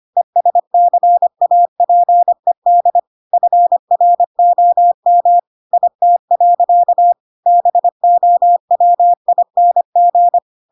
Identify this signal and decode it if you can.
Morse code, heard as a keyed tone, about 25 wpm, 700 Hz. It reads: ESCAPED FROM IT. BOWING
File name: Pinocchio / Pinocchio ch5 25 25 ik1hge